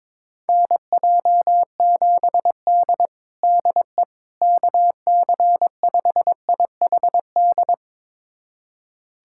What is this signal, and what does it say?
Morse code, heard as a keyed tone, about 22 wpm, 700 Hz. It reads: NJ7D DE KC5IHD
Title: USA Callsign DE Callsign 22wpm